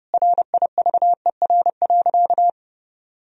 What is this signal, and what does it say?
Morse code, heard as a keyed tone, about 32 wpm, 700 Hz. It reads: RIVER.
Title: Sentences from Top 1000 Words - Review Part 1 - 30wpm